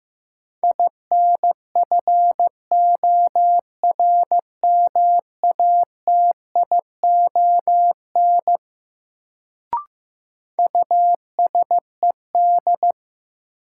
Morse code, heard as a keyed tone, about 15 wpm, 700 Hz. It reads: INFORMATION USED
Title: Sentences from Top 600 Words - New Word Drill - 15wpm